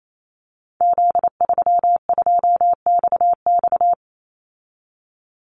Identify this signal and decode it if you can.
Morse code, heard as a keyed tone, about 28 wpm, 700 Hz. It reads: Z32XX
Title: Call Signs Repeated 2x - 28wpm